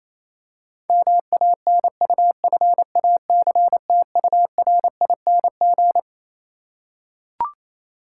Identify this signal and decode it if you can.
Morse code, heard as a keyed tone, about 28 wpm, 700 Hz. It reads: MANUFACTURING
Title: Practice QSO - Exchange 3 - More Info - 28wpm